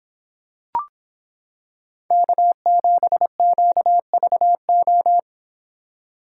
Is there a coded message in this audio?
Yes: Morse code from 2.1 to 5.2 s, heard as a keyed tone, about 26 wpm, 700 Hz. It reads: K7QVO